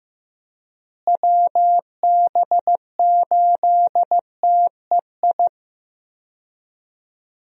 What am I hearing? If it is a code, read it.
Morse code, heard as a keyed tone, about 15 wpm, 700 Hz. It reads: WB8TEI
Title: USA Call Signs - 6 Characters - Repeated 2x - 15wpm